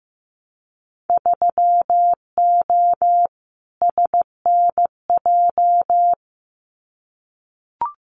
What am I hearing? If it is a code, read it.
Morse code, heard as a keyed tone, about 15 wpm, 700 Hz. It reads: 3O SNJ
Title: Winter Field Day Exchanges Encore 15wpm